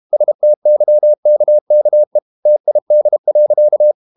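Morse code, heard as a keyed tone, about 32 wpm, 600 Hz. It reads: STYKKE TID.
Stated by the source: High-speed QRQ CW at 32 WPM.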